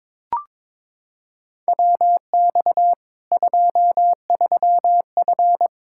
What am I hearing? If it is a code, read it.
Morse code, heard as a keyed tone, about 21 wpm, 700 Hz. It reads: WX 23F
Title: Practice QSO - Exchange 2 - Info - 22wpm